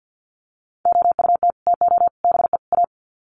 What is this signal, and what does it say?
Morse code, heard as a keyed tone, about 48 wpm, 700 Hz. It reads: OUT TO BE A
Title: Sets of 4 Words 50wpm